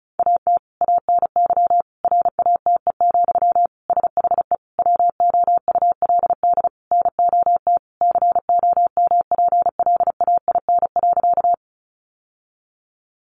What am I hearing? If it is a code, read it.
Morse code, heard as a keyed tone, about 34 wpm, 700 Hz. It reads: AT ANY RATE, SHE WOULD NOT COMPLAIN.
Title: Sentences from Top 1300 Words 35wpm